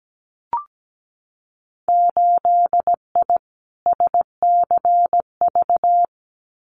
Morse code, heard as a keyed tone, about 17 wpm, 700 Hz. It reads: 8I SCV
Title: Winter Field Day Exchanges Encore 17wpm